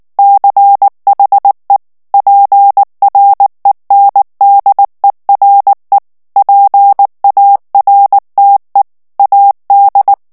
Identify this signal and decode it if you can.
Morse code, heard as a keyed tone, about 19 wpm, 800 Hz. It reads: CHE PRENDERE PARTE AD
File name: morse mp3 / ITALIEN texte clair 2 vitesse 18 mots minute